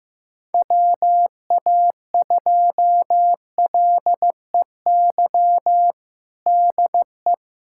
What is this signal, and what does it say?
Morse code, heard as a keyed tone, about 15 wpm, 700 Hz. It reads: WA2LEY DE
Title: USA Callsign DE Callsign 15wpm